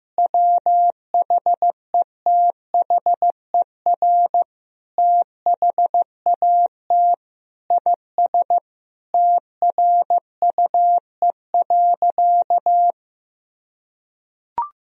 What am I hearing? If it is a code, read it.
Morse code, heard as a keyed tone, about 15 wpm, 700 Hz. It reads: WHETHER THAT IS TRUE.
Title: Sentences from Top 1000 Words - Review Part 4 - 15wpm